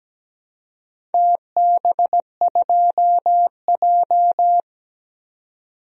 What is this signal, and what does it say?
Morse code, heard as a keyed tone, about 17 wpm, 700 Hz. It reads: TB2J
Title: DXCC-100 Call Signs - 4 Characters - 17wpm